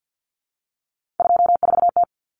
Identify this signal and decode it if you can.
Morse code, heard as a keyed tone, about 50 wpm, 700 Hz. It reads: 24T
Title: Common Cut Numbers - 3-Digit 50wpm